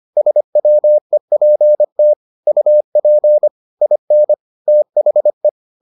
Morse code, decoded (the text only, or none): SWEPT UP IN THE